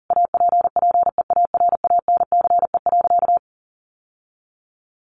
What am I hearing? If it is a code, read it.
Morse code, heard as a keyed tone, about 40 wpm, 700 Hz. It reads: APPEARANCE.
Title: Sentences from Top 1100 Words 40wpm